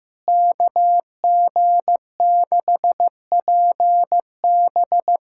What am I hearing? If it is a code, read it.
Morse code, heard as a keyed tone, about 15 wpm, 700 Hz. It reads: KG6PB
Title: K1USN SST QSO Practice - 15wpm